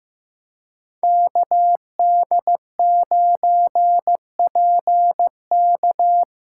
Morse code, heard as a keyed tone, about 15 wpm, 700 Hz. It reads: KD9PK
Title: USA Call Signs - 5 Characters - 3x Character Spacing - 15wpm